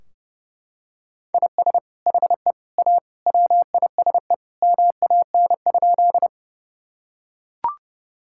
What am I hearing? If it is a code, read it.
Morse code, heard as a keyed tone, about 30 wpm, 700 Hz. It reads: IS HE A WISE MAN?
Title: Sentences from Top 1600 Words 30wpm